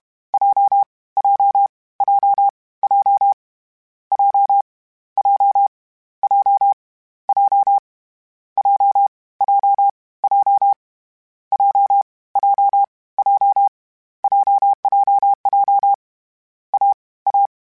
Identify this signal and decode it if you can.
Morse code, heard as a keyed tone, about 31 wpm, 800 Hz. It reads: J J J J J J J J J J J J J J JJJ A A
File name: Short Runs / Short Runs-mp3s / AWJ @30